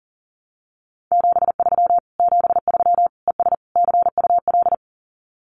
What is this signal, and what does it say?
Morse code, heard as a keyed tone, about 40 wpm, 700 Hz. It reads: 73 73 ES CUL
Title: Common QSO Phrases Encore 40wpm